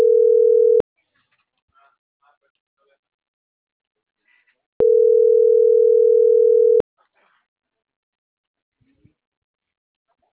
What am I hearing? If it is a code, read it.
Ringback tone (North America), 440 + 480 Hz.